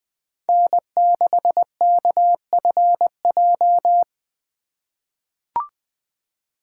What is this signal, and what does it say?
Morse code, heard as a keyed tone, about 20 wpm, 700 Hz. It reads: N6KFJ
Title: USA Call Signs - 5 Characters - 2x Character Spacing - 20wpm